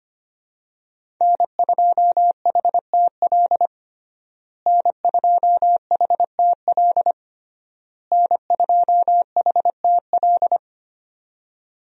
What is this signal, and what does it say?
Morse code, heard as a keyed tone, about 25 wpm, 700 Hz. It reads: N2HTL N2HTL N2HTL
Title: USA Call Signs - 5 Characters - Repeated 3x - 25wpm